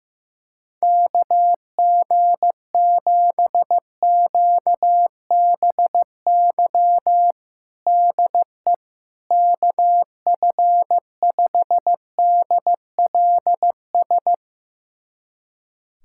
Morse code, heard as a keyed tone, about 16 wpm, 700 Hz. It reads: KG7QBY DE KF5DLS
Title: USA Callsign DE Callsign - 6 Characters - 15wpm